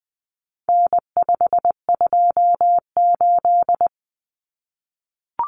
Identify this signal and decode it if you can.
Morse code, heard as a keyed tone, about 20 wpm, 700 Hz. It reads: N528
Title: Common Cut Numbers - 4-Digit 20wpm